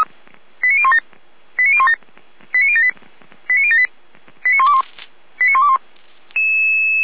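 Five-tone selective call (ZVEI), heard as quick runs of five tones, tones of about 70 ms. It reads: 89017 89017 89078 89078 89121 89121
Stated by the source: ZVEI five-tone selective call received on an SDR by feuerwehr-orlamuende fire brigade.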